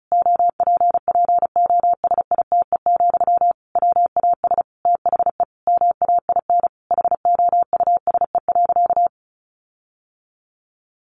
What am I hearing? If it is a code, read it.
Morse code, heard as a keyed tone, about 34 wpm, 700 Hz. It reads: OPPOSITE, WAS THE MAIN HOUSE.
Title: Sentences from Top 1300 Words 35wpm